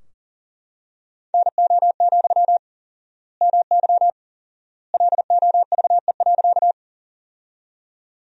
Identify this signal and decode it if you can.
Morse code, heard as a keyed tone, about 40 wpm, 700 Hz. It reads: NO, MY LOVE.
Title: 4x Word-Spacing - Sentences from Top 400 Words - 40wpm